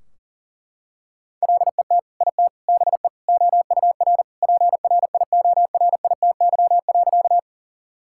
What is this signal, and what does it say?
Morse code, heard as a keyed tone, about 40 wpm, 700 Hz. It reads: LET IT BE OUR PRIORITY.
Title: Sentences from Top 1600 Words 40wpm